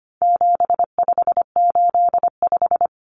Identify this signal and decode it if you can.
Morse code, heard as a keyed tone, about 25 wpm, 700 Hz. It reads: 7585
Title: Common Cut Numbers - 4-Digit 25wpm